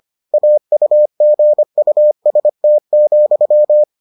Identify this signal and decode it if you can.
Morse code, heard as a keyed tone, about 25 wpm, 600 Hz. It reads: AUGUST,